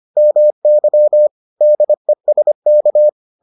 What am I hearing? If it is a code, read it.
Morse code, heard as a keyed tone, about 25 wpm, 600 Hz. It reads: MY DESK